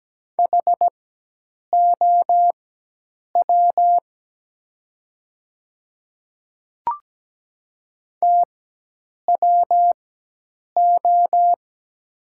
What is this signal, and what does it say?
Morse code, heard as a keyed tone, about 17 wpm, spaced as for 8 wpm, 700 Hz. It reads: HOW TWO
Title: 4x Character Spacing - Top 100 Words - 17wpm